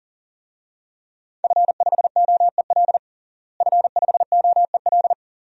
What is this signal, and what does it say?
Morse code, heard as a keyed tone, about 40 wpm, 700 Hz. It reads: F5OEL F5OEL
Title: DXCC-100 Call Signs - 5 Characters - Repeated 2x - 40wpm